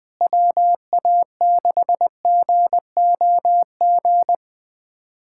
Morse code, heard as a keyed tone, about 20 wpm, 700 Hz. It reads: WA6GOG
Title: USA Callsign DE Callsign 20wpm